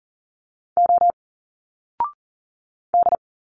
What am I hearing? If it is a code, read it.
Morse code, heard as a keyed tone, about 40 wpm, 700 Hz. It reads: O D
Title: Single Letters 40wpm